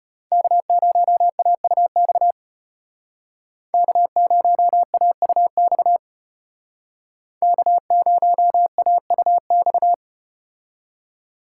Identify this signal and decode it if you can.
Morse code, heard as a keyed tone, about 34 wpm, 700 Hz. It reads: K0AUX K0AUX K0AUX